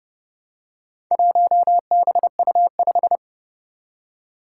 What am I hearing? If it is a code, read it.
Morse code, heard as a keyed tone, about 30 wpm, 700 Hz. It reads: 1BU5
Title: Four Challenging Letter-Number Combinations 30wpm